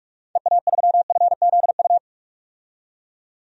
Morse code, heard as a keyed tone, about 46 wpm, 700 Hz. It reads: EA3FZU
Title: DXCC-100 Call Signs - 6 Characters - 45wpm